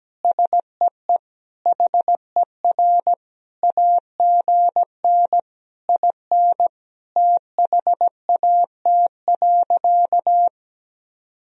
Morse code, heard as a keyed tone, about 17 wpm, 700 Hz. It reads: SEE HER AGN IN THAT.
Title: Sentences from Top 200 Words and Abbreviations 17wpm